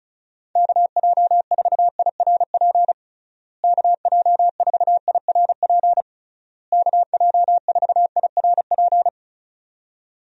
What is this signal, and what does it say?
Morse code, heard as a keyed tone, about 36 wpm, 700 Hz. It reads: KJ4IRP KJ4IRP KJ4IRP
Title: USA Call Signs - 6 Characters - Repeated 3x - 35wpm